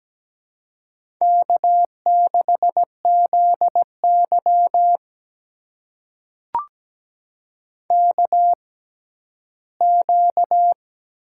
Morse code, heard as a keyed tone, about 17 wpm, spaced as for 11 wpm, 700 Hz. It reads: K6ZY KQ
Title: USA Call Signs - 4 Characters - 6x Character Spacing - 17wpm